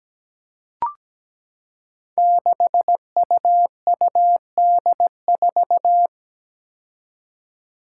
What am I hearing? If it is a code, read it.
Morse code, heard as a keyed tone, about 17 wpm, 700 Hz. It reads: 6UUD4
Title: Five Challenging Letter-Number Combinations 17wpm